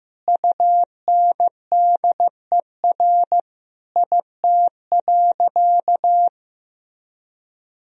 Morse code, heard as a keyed tone, about 15 wpm, 700 Hz. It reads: UNDER IT.
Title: Sentences from Top 1000 Words - Review Part 1 - 15wpm